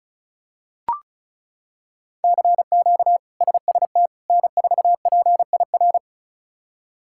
Morse code, heard as a keyed tone, about 36 wpm, 700 Hz. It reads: CQ SST N4PIR
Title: K1USN SST QSO Practice - 35wpm